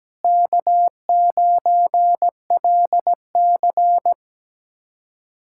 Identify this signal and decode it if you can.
Morse code, heard as a keyed tone, about 17 wpm, 700 Hz. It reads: K9LC
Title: K1USN SST QSO Practice - 17wpm